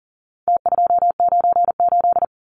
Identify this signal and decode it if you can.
Morse code, heard as a keyed tone, about 40 wpm, 700 Hz. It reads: T298
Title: Numbers - Contest Style - 4-Digit 40wpm